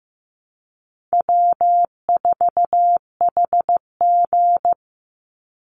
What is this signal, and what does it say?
Morse code, heard as a keyed tone, about 15 wpm, 700 Hz. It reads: W4HG
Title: Easy Call Signs Repeated 2x - 15wpm